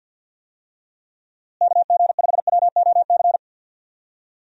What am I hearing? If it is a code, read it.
Morse code, heard as a keyed tone, about 52 wpm, 700 Hz. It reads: KG5WOC
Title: USA Call Signs - 6 Characters - Repeated 3x - 50wpm